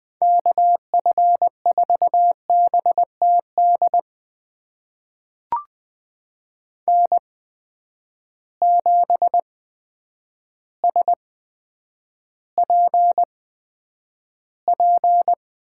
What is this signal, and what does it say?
Morse code, heard as a keyed tone, about 20 wpm, spaced as for 9 wpm, 700 Hz. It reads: KF4BTD N7SPP
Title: USA Call Signs - 8x Character Spacing - 20wpm